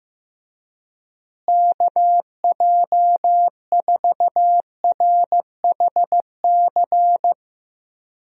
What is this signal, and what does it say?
Morse code, heard as a keyed tone, about 15 wpm, 700 Hz. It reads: KJ4RHC
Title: USA Call Signs - 6 Characters - Repeated 2x - 15wpm